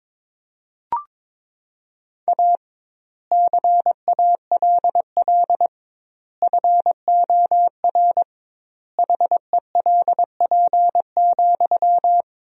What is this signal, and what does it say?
Morse code, heard as a keyed tone, about 22 wpm, 700 Hz. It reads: A CALL FOR HELP,
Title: 2x Word-Spacing - Sentences from Top 300 Words - 22wpm